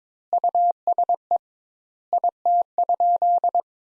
Morse code, heard as a keyed tone, about 22 wpm, 700 Hz. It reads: USE IT?
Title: Sentences Top 200 Words - 22wpm